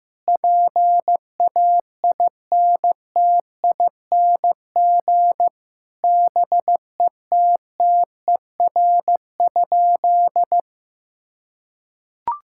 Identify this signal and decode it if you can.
Morse code, heard as a keyed tone, about 15 wpm, 700 Hz. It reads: PAINTING BETTER?
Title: Sentences from Top 1000 Words - Review Part 6 - 15wpm